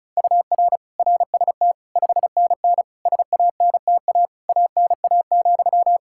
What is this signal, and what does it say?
Morse code, heard as a keyed tone, about 36 wpm, 700 Hz. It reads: UR RST 5NN SANTA ANA,
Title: Practice QSO - Exchange 1 - Beginning - 35wpm